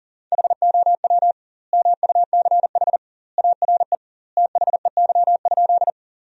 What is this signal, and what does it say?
Morse code, heard as a keyed tone, about 40 wpm, 700 Hz. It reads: HOW MUCH ARE THEY?
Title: 2x Word-Spacing - Sentences from Top 200 Words - 40wpm